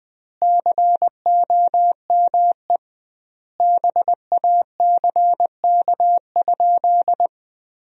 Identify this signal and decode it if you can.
Morse code, heard as a keyed tone, about 20 wpm, 700 Hz. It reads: COME BACK?
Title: 2x Word-Spacing - Sentences from Top 300 Words - 20wpm